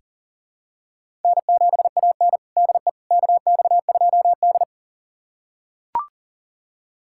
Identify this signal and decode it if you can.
Morse code, heard as a keyed tone, about 40 wpm, 700 Hz. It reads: N7AN DE KX2D
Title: USA Callsign DE Callsign - 4 Characters - 40wpm